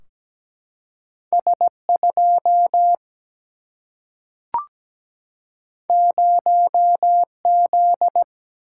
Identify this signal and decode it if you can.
Morse code, heard as a keyed tone, about 17 wpm, 700 Hz. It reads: S2 0Z
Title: Single-Letter Single-Number 17wpm